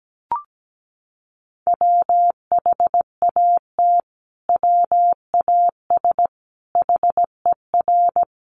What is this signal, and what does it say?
Morse code, heard as a keyed tone, about 17 wpm, 700 Hz. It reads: WHAT WAS HER
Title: Sentences from Top 1400 Words 17wpm